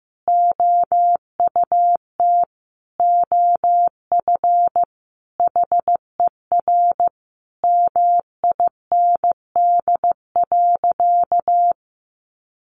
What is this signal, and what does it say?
Morse code, heard as a keyed tone, about 15 wpm, 700 Hz. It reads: OUT OF HER MIND.
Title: Sentences from Top 1300 Words 15wpm